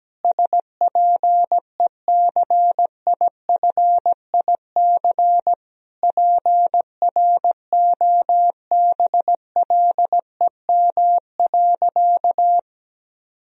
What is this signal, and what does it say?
Morse code, heard as a keyed tone, about 17 wpm, 700 Hz. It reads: SPECIFIC PROBLEM.
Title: Sentences from Top 1000 Words - Review Part 3 - 17wpm